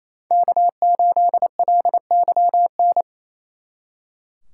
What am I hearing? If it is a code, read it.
Morse code, heard as a keyed tone, about 34 wpm, 700 Hz. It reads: K8LYN
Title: Speed Racing - USA 1x3 Call Signs 28wpm